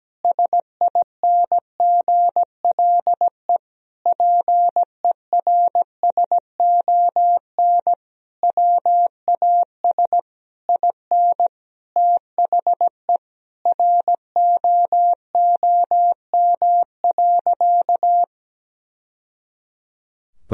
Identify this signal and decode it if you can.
Morse code, heard as a keyed tone, about 17 wpm, 700 Hz. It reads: SINGLE PERSON WAS IN THE ROOM.
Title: Sentences from Top 1000 Words - Review Part 4 - 17wpm